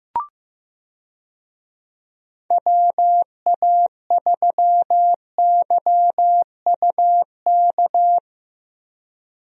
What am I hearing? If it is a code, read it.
Morse code, heard as a keyed tone, about 15 wpm, 700 Hz. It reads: WA3YUK
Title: USA Call Signs - 6 Characters - Repeated 2x - 15wpm